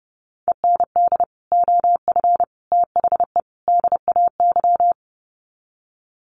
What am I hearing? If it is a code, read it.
Morse code, heard as a keyed tone, about 30 wpm, 700 Hz. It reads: END OF THE DAY
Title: Sets of 4 Words 30wpm